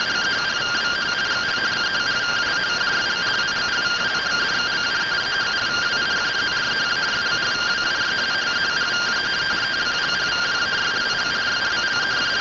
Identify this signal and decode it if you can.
Two-tone teleprinter signal, about 170 Hz shift.
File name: findings / CT AXIAL -572mm 2.gen-40 rtty rx